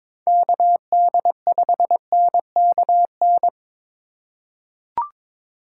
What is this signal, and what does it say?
Morse code, heard as a keyed tone, about 22 wpm, 700 Hz. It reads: KD5NKN
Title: USA Call Signs - 2x Character Spacing - 22wpm